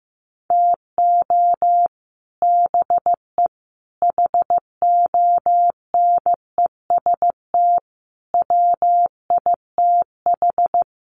Morse code, heard as a keyed tone, about 15 wpm, 700 Hz. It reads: TO BE HONEST WITH
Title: Sets of 5 Words Encore 15wpm